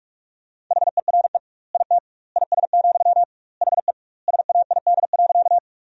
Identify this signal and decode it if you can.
Morse code, heard as a keyed tone, about 41 wpm, 700 Hz. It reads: HERE IT IS, HE SAID.
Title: 2x Word-Spacing - Sentences from Top 200 Words - 45wpm